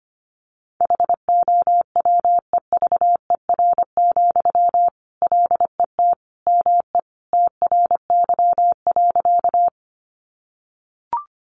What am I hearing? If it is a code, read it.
Morse code, heard as a keyed tone, about 25 wpm, 700 Hz. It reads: HOWEVER, LET ME TRY.
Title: Sentences from Top 600 Words 25wpm